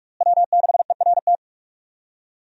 Morse code, heard as a keyed tone, about 46 wpm, 700 Hz. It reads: W6ERT
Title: USA Call Signs - 5 Characters - Repeated 2x - 45wpm